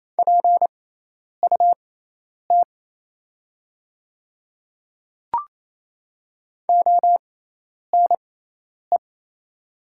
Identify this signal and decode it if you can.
Morse code, heard as a keyed tone, about 28 wpm, spaced as for 10 wpm, 700 Hz. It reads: PUT ONE
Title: 6x Character Spacing - Top 200 Words - 28wpm